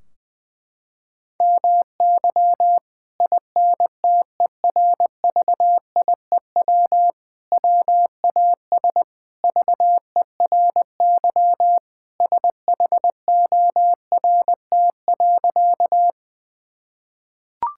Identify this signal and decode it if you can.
Morse code, heard as a keyed tone, about 20 wpm, 700 Hz. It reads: MY INTERVIEW WAS VERY SHORT.
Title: Sentences from Top 1600 Words 20wpm